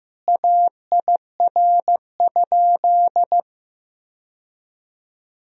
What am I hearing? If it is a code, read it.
Morse code, heard as a keyed tone, about 16 wpm, 700 Hz. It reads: AIR?
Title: Sentences from Top 1000 Words - Review Part 2 - 15wpm